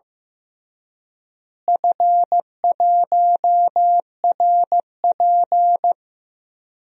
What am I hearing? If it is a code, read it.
Morse code, heard as a keyed tone, about 15 wpm, 700 Hz. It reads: F1RP
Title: DXCC-100 Call Signs - 4 Characters - Repeated 3x - 15wpm